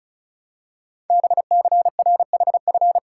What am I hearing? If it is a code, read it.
Morse code, heard as a keyed tone, about 36 wpm, 690 Hz. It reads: BCRHF
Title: Code Groups - 5 Letters - 35wpm